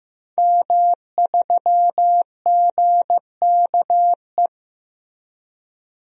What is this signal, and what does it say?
Morse code, heard as a keyed tone, about 16 wpm, 700 Hz. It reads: M3GKE
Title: DXCC-100 Call Signs - 5 Characters - 15wpm